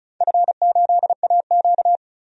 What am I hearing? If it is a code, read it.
Morse code, heard as a keyed tone, about 34 wpm, 700 Hz. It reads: F8AQ
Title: DXCC-100 Call Signs - 4 Characters - 35wpm